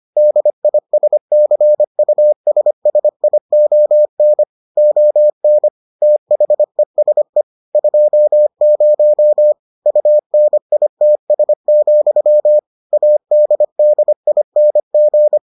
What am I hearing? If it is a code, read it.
Morse code, heard as a keyed tone, about 25 wpm, 600 Hz. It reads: DISCUSSION ON THESE 20 UNITS, ADDING